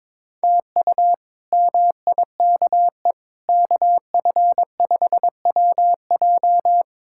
Morse code, heard as a keyed tone, about 22 wpm, 700 Hz. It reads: TU MIKE KF5WJ
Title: K1USN SST QSO Practice - 22wpm